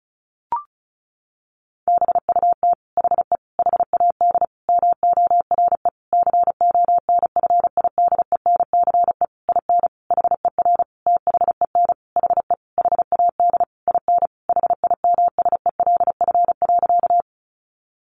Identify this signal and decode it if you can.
Morse code, heard as a keyed tone, about 34 wpm, 700 Hz. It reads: BUT HE HAD MORE CONFIDENCE IN HER THEN HE HAD IN HIMSELF.
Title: Sentences from Top 1500 Words 35wpm